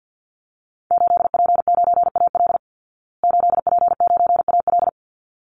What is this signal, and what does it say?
Morse code, heard as a keyed tone, about 50 wpm, 700 Hz. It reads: 8P9AL 8P9AL
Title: Call Signs Repeated 3x - 50wpm